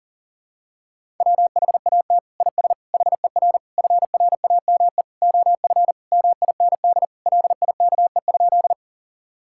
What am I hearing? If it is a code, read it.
Morse code, heard as a keyed tone, about 40 wpm, 700 Hz. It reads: WHAT IS HER FRAME OF MIND LIKE?
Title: Sentences from Top 1000 Words - Review Part 6 - 40wpm